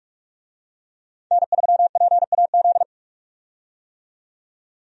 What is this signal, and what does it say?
Morse code, heard as a keyed tone, about 46 wpm, 700 Hz. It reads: N3PAZ
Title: USA Call Signs - 5 Characters - 45wpm